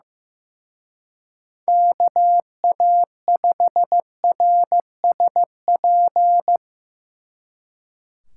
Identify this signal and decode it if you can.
Morse code, heard as a keyed tone, about 17 wpm, 700 Hz. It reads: KA5RSP
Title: Speed Racing - USA Call Signs 15wpm